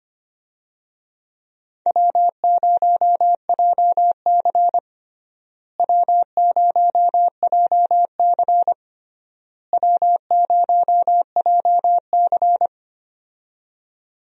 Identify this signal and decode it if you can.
Morse code, heard as a keyed tone, about 25 wpm, 700 Hz. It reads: W0JC W0JC W0JC
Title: USA Call Signs - 4 Characters - Repeated 3x - 25wpm